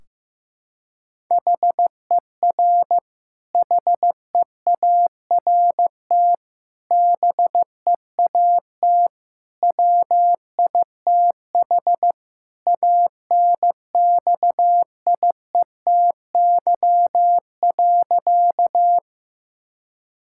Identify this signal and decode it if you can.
Morse code, heard as a keyed tone, about 16 wpm, 700 Hz. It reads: HER HEART BEAT WITH ANXIETY.
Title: Sentences from Top 1600 Words 15wpm